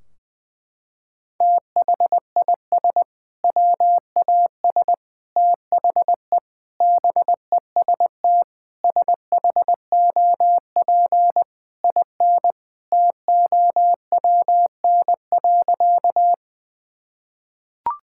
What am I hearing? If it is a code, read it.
Morse code, heard as a keyed tone, about 21 wpm, 700 Hz. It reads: THIS WAS THE BEST SHOP IN TOWN.
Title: Sentences from Top 1600 Words 20wpm